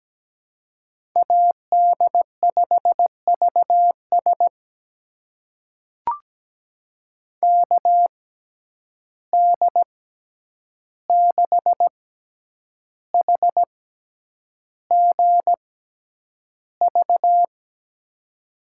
Morse code, heard as a keyed tone, about 17 wpm, spaced as for 8 wpm, 700 Hz. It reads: AD5VS KD6HGV
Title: USA Call Signs - 6x Character Spacing - 17wpm